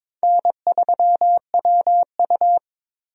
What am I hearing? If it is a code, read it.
Morse code, heard as a keyed tone, about 22 wpm, 700 Hz. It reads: N3WU
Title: USA Call Signs - 4 Characters - Repeated 2x - 22wpm